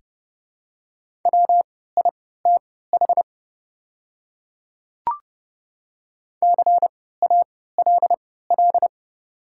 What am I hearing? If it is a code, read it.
Morse code, heard as a keyed tone, about 30 wpm, spaced as for 17 wpm, 700 Hz. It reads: WITH CALL
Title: 3x Character Spacing - Top 100 Words - 30wpm